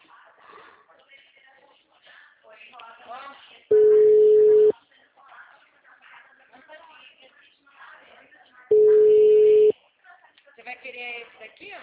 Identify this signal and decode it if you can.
Ringback tone (Europe, 425 Hz), with speech.
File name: out- -2007- - - .9063